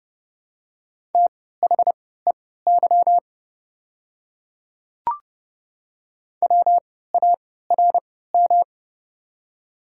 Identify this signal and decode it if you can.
Morse code, heard as a keyed tone, about 29 wpm, spaced as for 17 wpm, 700 Hz. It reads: THEY WARM
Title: 3x Character Spacing - Top 500 Words - 30wpm